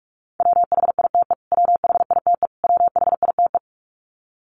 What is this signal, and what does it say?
Morse code, heard as a keyed tone, about 44 wpm, 700 Hz. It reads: WHITE WHITE WHITE
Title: Top 300 Words Repeated 45wpm